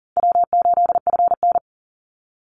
Morse code, heard as a keyed tone, about 40 wpm, 700 Hz. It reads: W8FN
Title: Call Signs Repeated 2x - 40wpm